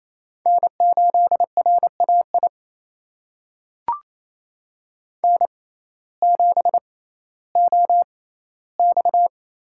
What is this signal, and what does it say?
Morse code, heard as a keyed tone, about 27 wpm, spaced as for 15 wpm, 700 Hz. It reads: N8RAI N7OX
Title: USA Call Signs - 6x Character Spacing - 28wpm